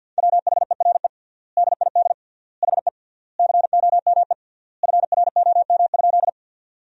Morse code, heard as a keyed tone, about 48 wpm, 700 Hz. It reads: WHERE DID HE COME FROM?
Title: 3x Word-Spacing - Sentences from Top 200 Words - 50wpm